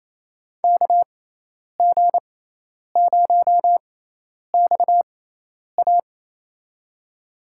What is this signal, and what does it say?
Morse code, heard as a keyed tone, about 28 wpm, spaced as for 10 wpm, 700 Hz. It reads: KG0XA
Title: USA Call Signs - 6x Character Spacing - 28wpm